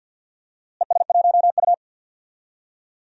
Morse code, heard as a keyed tone, about 47 wpm, 700 Hz. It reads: ES1U
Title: DXCC-100 Call Signs - 4 Characters - Repeated 3x - 50wpm